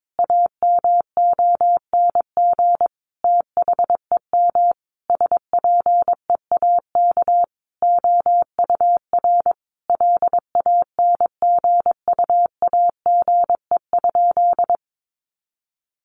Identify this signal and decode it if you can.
Morse code, heard as a keyed tone, about 22 wpm, 700 Hz. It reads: AMONG THEM SPEAK OUR LANGUAGE?
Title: Sentences from Top 1100 Words 22wpm